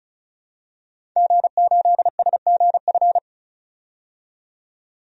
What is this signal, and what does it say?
Morse code, heard as a keyed tone, about 36 wpm, 700 Hz. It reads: G8SGF
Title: DXCC-100 Call Signs - 5 Characters - 35wpm